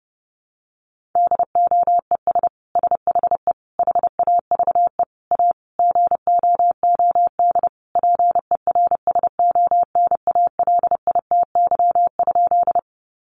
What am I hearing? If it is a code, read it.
Morse code, heard as a keyed tone, about 30 wpm, 700 Hz. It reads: DOES SHE HAVE A GOOD PERSONALITY?
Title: Sentences from Top 1400 Words 30wpm